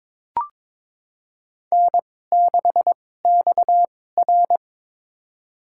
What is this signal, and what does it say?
Morse code, heard as a keyed tone, about 22 wpm, spaced as for 16 wpm, 700 Hz. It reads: N6XR
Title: USA Call Signs - 2x Character Spacing - 22wpm